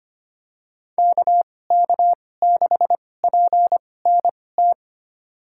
Morse code, heard as a keyed tone, about 25 wpm, spaced as for 18 wpm, 700 Hz. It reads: KK6PNT
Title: USA Call Signs - 2x Character Spacing - 25wpm